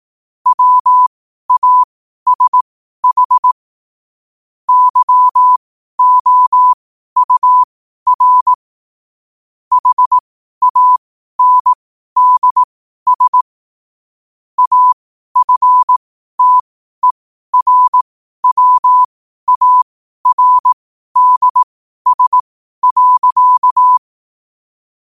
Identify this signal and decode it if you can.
Morse code, heard as a keyed tone, about 18 wpm, spaced as for 12 wpm, 1000 Hz. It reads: WASH YOUR HANDS AFTERWARDS.
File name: qotdfiles / 12wpm